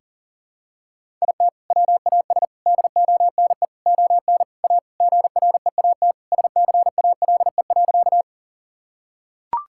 Morse code, heard as a keyed tone, about 40 wpm, 700 Hz. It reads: IT WAS DONE ON A GREAT SCALE.
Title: Sentences from Top 1000 Words - Review Part 5 - 40wpm